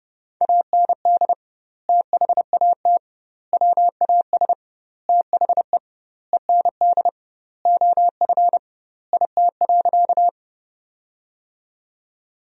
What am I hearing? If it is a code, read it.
Morse code, heard as a keyed tone, about 32 wpm, 700 Hz. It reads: AND THAT WAS THE END OF IT.
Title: 2x Word-Spacing - Sentences from Top 200 Words - 30wpm